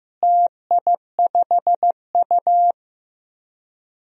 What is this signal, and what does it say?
Morse code, heard as a keyed tone, about 15 wpm, 700 Hz. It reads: TI5U
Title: DXCC-100 Call Signs - 4 Characters - Repeated 3x - 15wpm